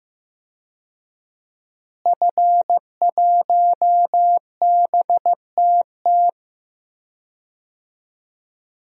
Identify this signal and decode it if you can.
Morse code, heard as a keyed tone, about 15 wpm, 700 Hz. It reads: F1BTT
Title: DXCC-100 Call Signs - 5 Characters - Repeated 3x - 15wpm